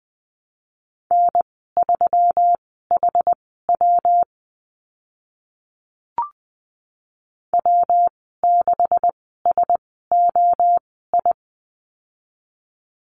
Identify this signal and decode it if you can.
Morse code, heard as a keyed tone, about 20 wpm, spaced as for 14 wpm, 700 Hz. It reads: N3HW W6SOI
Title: 140 Lesson 23g - Letter f - Call Signs 20wpm